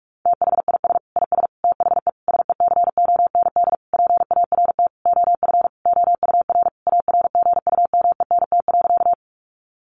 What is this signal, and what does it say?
Morse code, heard as a keyed tone, about 44 wpm, 700 Hz. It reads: THIS IS THE SECOND PART OF OUR ARGUMENT.
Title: Sentences from Top 1400 Words 45wpm